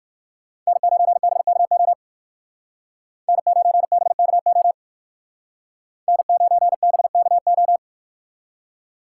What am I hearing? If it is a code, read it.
Morse code, heard as a keyed tone, about 57 wpm, 700 Hz. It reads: N9BKO N9BKO N9BKO